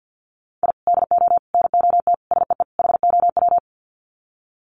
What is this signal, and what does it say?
Morse code, heard as a keyed tone, about 50 wpm, 700 Hz. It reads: I DO NOT SEE HOW
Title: Sets of 5 Words 50wpm